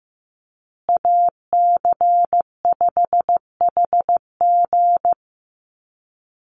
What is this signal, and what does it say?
Morse code, heard as a keyed tone, about 15 wpm, 700 Hz. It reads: AC5HG
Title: USA Call Signs Repeated 2x 15wpm